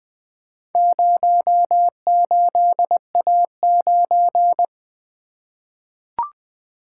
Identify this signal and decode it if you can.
Morse code, heard as a keyed tone, about 20 wpm, 700 Hz. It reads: 08A9
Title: Four Letter-Number Combinations - 20wpm